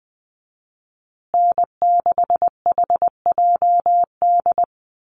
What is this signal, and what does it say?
Morse code, heard as a keyed tone, about 20 wpm, 700 Hz. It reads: N6HJD
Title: USA Call Signs Repeated 3x 20wpm